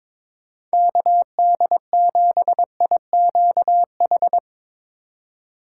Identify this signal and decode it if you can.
Morse code, heard as a keyed tone, about 24 wpm, 700 Hz. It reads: KD7IQH
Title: Speed Racing - USA 2x3 Call Signs 22wpm